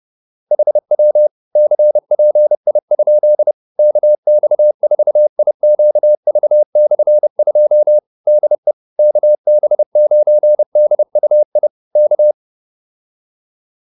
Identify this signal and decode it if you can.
Morse code, heard as a keyed tone, about 30 wpm, 600 Hz. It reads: HW CPI? KX4IQV/2 DE KB9DUI K